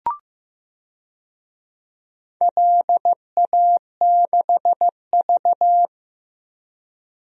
Morse code, heard as a keyed tone, about 15 wpm, 700 Hz. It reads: LA6V
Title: DXCC-100 Call Signs - 4 Characters - Repeated 2x - 15wpm